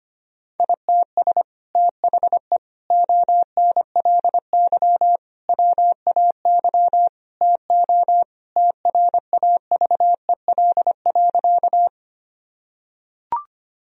Morse code, heard as a keyed tone, about 25 wpm, 700 Hz. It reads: ITS THE ONLY WAY TO TRAVEL.
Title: Sentences from Top 1000 Words - Review Part 2 - 25wpm